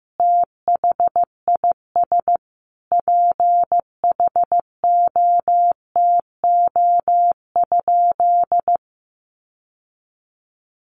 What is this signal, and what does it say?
Morse code, heard as a keyed tone, about 15 wpm, 700 Hz. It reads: THIS PHOTO?
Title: Sentences from Top 1200 Words 15wpm